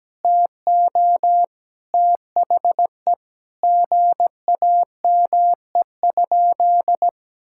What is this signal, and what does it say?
Morse code, heard as a keyed tone, about 17 wpm, 700 Hz. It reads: TO THE GAME?
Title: Sentences from Top 1600 Words 17wpm